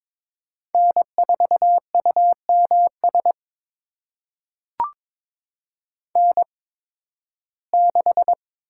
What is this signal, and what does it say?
Morse code, heard as a keyed tone, about 22 wpm, spaced as for 14 wpm, 700 Hz. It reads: N4UMS N6
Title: USA Call Signs - 8x Character Spacing - 22wpm